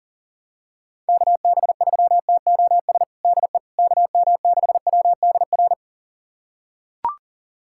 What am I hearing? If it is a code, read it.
Morse code, heard as a keyed tone, about 40 wpm, 700 Hz. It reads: KB3TOS DE KM6WDR
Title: USA Callsign DE Callsign - 6 Characters - 40wpm